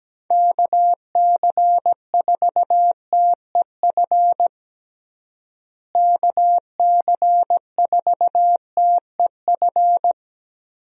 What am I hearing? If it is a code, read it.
Morse code, heard as a keyed tone, about 17 wpm, 700 Hz. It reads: KC4TEF KC4TEF
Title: USA Call Signs - 6 Characters - Repeated 2x - 17wpm